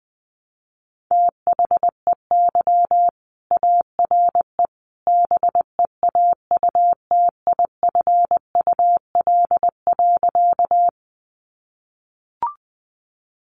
Morse code, heard as a keyed tone, about 20 wpm, 700 Hz. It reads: THEY ARE BEAUTIFUL.
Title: Sentences from Top 900 Words 20wpm